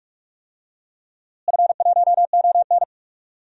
Morse code, heard as a keyed tone, about 42 wpm, 700 Hz. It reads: F1ON
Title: DXCC-100 Call Signs - 4 Characters - Repeated 3x - 45wpm